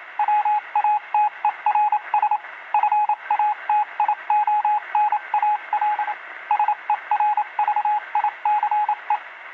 Morse code, heard as a keyed tone, about 27 wpm, 840 Hz, with radio noise. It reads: WATERS FATIONAL SERVICE